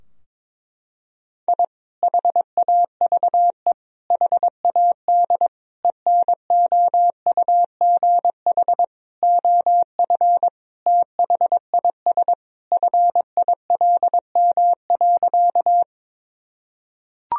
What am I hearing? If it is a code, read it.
Morse code, heard as a keyed tone, about 21 wpm, 700 Hz. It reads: I HAVE HAD ENOUGH OF THIS FILM.
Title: Sentences from Top 1000 Words - Review Part 3 - 22wpm